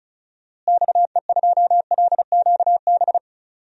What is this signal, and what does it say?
Morse code, heard as a keyed tone, about 36 wpm, 700 Hz. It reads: XE2LQB
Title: DXCC-100 Call Signs - 6 Characters - Repeated 3x - 35wpm